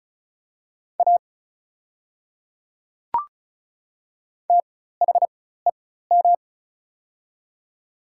Morse code, heard as a keyed tone, about 36 wpm, spaced as for 16 wpm, 700 Hz. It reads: A THEM
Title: 4x Character Spacing - Top 200 Words - 35wpm